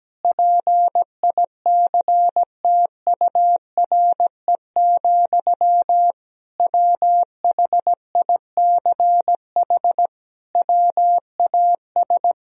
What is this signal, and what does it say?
Morse code, heard as a keyed tone, about 17 wpm, 700 Hz. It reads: PICTURE, WHICH WAS
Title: Sentences from Top 1000 Words - Review Part 3 - 17wpm